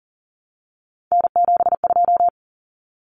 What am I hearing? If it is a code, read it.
Morse code, heard as a keyed tone, about 40 wpm, 700 Hz. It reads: N72
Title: Common Cut Numbers - 3-Digit 40wpm